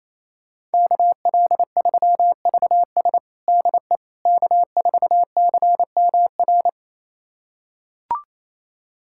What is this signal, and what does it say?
Morse code, heard as a keyed tone, about 28 wpm, 700 Hz. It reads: KL3VS DE K4CMR
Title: USA Callsign DE Callsign - 5 Characters - 28wpm